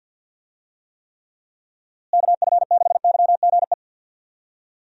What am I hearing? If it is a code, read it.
Morse code, heard as a keyed tone, about 52 wpm, 700 Hz. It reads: KF6YGE
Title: USA Call Signs - 6 Characters - Repeated 2x - 50wpm